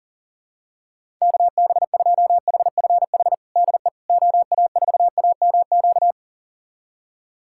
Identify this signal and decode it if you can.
Morse code, heard as a keyed tone, about 40 wpm, 700 Hz. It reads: KB2HFH DE OA4AMQ
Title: USA Callsign DE Callsign - 6 Characters - 40wpm